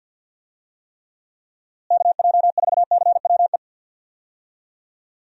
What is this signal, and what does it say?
Morse code, heard as a keyed tone, about 47 wpm, 700 Hz. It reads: KJ4CWE
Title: USA Call Signs - 6 Characters - Repeated 2x - 50wpm